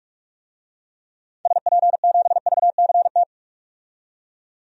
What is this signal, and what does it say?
Morse code, heard as a keyed tone, about 42 wpm, 700 Hz. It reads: SP7VCT
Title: DXCC-100 Call Signs - 6 Characters - 45wpm